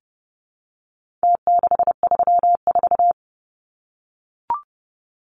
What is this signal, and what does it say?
Morse code, heard as a keyed tone, about 30 wpm, 700 Hz. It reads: T634
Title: Numbers - Contest Style - 4-Digit 30wpm